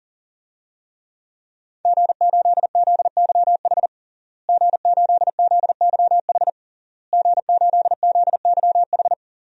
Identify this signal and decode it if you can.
Morse code, heard as a keyed tone, about 40 wpm, 700 Hz. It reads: G8ZYH G8ZYH G8ZYH
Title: DXCC-100 Call Signs - 5 Characters - Repeated 3x - 40wpm